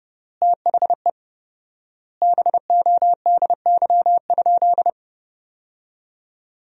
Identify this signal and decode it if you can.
Morse code, heard as a keyed tone, about 30 wpm, 700 Hz. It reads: THE BODY?
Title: 4x Word-Spacing - Sentences from Top 400 Words - 30wpm